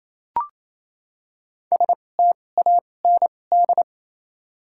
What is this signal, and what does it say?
Morse code, heard as a keyed tone, about 28 wpm, spaced as for 20 wpm, 700 Hz. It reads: STAND
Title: 2x Character Spacing - Top 500 Words - 28wpm